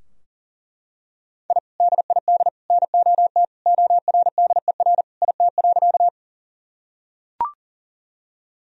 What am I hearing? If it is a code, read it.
Morse code, heard as a keyed tone, about 37 wpm, 700 Hz. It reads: I DID NOT ORDER IT.
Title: Sentences from Top 1000 Words - Review Part 2 - 40wpm